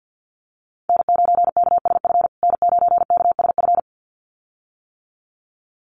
Morse code, heard as a keyed tone, about 50 wpm, 700 Hz. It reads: N9KSF N9KSF
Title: USA Call Signs Repeated 2x 50wpm